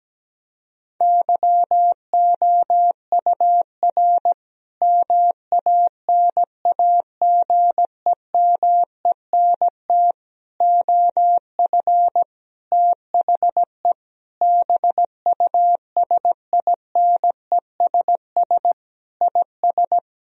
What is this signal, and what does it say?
Morse code, heard as a keyed tone, about 17 wpm, 700 Hz. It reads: YOUR MANAGEMENT OF THE BUSINESS IS
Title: Sentences from Top 1000 Words - Review Part 4 - 17wpm